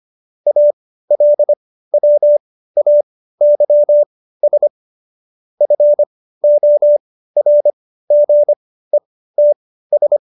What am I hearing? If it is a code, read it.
Morse code, heard as a keyed tone, about 25 wpm, spaced as for 15 wpm, 590 Hz. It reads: ALWAYS FORGETS
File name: Jokes-25-char-15-eff